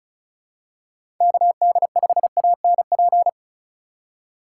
Morse code, heard as a keyed tone, about 36 wpm, 700 Hz. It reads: KD5ANP
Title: USA Call Signs - 6 Characters - 4x Character Spacing - 35wpm